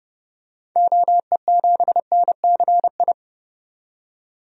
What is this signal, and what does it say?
Morse code, heard as a keyed tone, about 30 wpm, 700 Hz. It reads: OE7NCI
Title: DXCC-100 Call Signs - 6 Characters - 30wpm